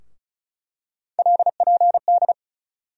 Morse code, heard as a keyed tone, about 34 wpm, 700 Hz. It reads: LPD
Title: USA Call Sign Postfixes - 3 Letters 35wpm